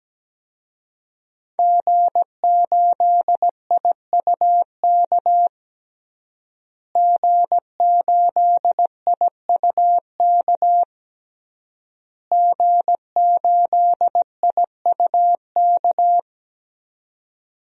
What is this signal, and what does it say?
Morse code, heard as a keyed tone, about 17 wpm, 700 Hz. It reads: G8IUK G8IUK G8IUK
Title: DXCC-100 Call Signs - 5 Characters - Repeated 3x - 17wpm